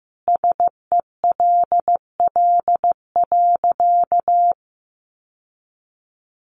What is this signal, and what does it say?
Morse code, heard as a keyed tone, about 15 wpm, 700 Hz. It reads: SELL.
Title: Sentences from Top 1100 Words 15wpm